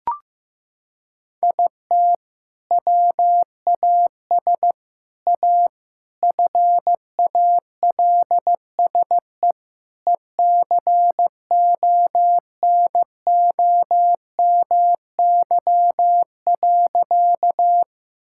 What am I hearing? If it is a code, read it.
Morse code, heard as a keyed tone, about 15 wpm, 700 Hz. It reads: IT WAS A FALSE ECONOMY.
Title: Sentences from Top 1600 Words 15wpm